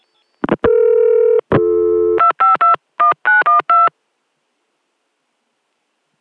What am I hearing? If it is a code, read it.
DTMF, heard as telephone touch tones, tones of about 140 ms. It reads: 2521912